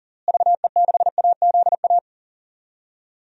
Morse code, heard as a keyed tone, about 40 wpm, 700 Hz. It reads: VE6AZA
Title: DXCC-100 Call Signs - 6 Characters - Repeated 3x - 40wpm